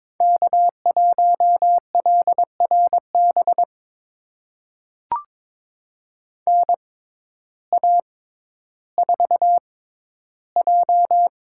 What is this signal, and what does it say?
Morse code, heard as a keyed tone, about 22 wpm, spaced as for 12 wpm, 700 Hz. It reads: K1LRB NA4J
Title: USA Call Signs - 6x Character Spacing - 22wpm